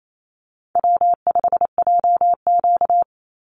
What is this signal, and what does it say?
Morse code, heard as a keyed tone, about 28 wpm, 700 Hz. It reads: W5JQ
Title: Call Signs Repeated 3x - 28wpm